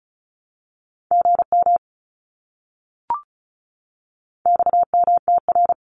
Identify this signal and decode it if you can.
Morse code, heard as a keyed tone, about 34 wpm, 700 Hz. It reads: GM XMTR
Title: Abbreviations Repeated 3x - 35wpm